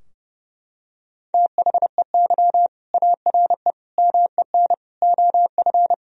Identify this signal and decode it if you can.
Morse code, heard as a keyed tone, about 30 wpm, 700 Hz. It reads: THEY ARE MEN OF
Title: Sentences from Top 1000 Words - Review Part 2 - 30wpm